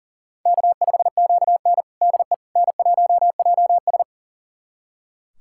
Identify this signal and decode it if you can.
Morse code, heard as a keyed tone, about 40 wpm, 700 Hz. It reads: K5QN DE N1JS
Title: USA Callsign DE Callsign - 4 Characters - 40wpm